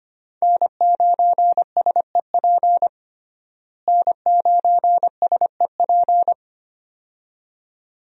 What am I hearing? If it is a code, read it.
Morse code, heard as a keyed tone, about 25 wpm, 700 Hz. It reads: N9SEP N9SEP
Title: USA Call Signs - 5 Characters - Repeated 2x - 25wpm